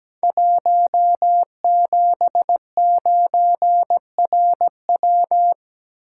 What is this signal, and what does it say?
Morse code, heard as a keyed tone, about 17 wpm, 700 Hz. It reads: 179RW
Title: Code Groups - 5 Letter:Numbers - 17wpm